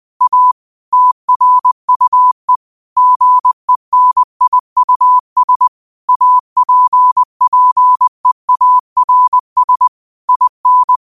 Morse code, heard as a keyed tone, about 20 wpm, 1000 Hz. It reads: A TRUE GENIUS APPEARS IN
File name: qotdfiles / 20wpm